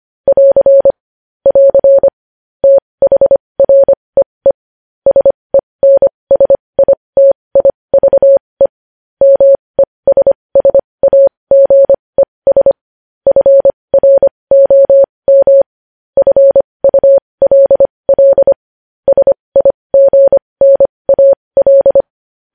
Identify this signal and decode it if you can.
Morse code, heard as a keyed tone, about 25 wpm, spaced as for 20 wpm, 550 Hz. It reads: <AR> <AR> THREE SENSITIVE MESSAGES FROM FULL SIGNAL
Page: News Headlines in Morse Code at 20 WPM